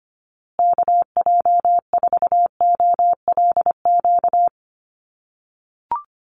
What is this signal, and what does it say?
Morse code, heard as a keyed tone, about 25 wpm, 700 Hz. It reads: KJ4OLQ
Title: USA Call Signs Repeated 2x 25wpm